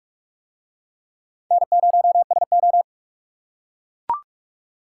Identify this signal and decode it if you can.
Morse code, heard as a keyed tone, about 42 wpm, 700 Hz. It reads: N0SO